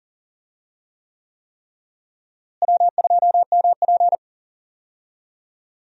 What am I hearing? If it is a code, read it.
Morse code, heard as a keyed tone, about 37 wpm, 700 Hz. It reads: W2MP
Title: USA Call Signs - 4 Characters - Repeated 3x - 40wpm